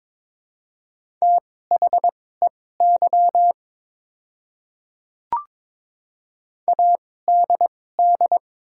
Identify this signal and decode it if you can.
Morse code, heard as a keyed tone, about 22 wpm, spaced as for 16 wpm, 700 Hz. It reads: THEY ADD
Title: 2x Character Spacing - Top 300 Words - 22wpm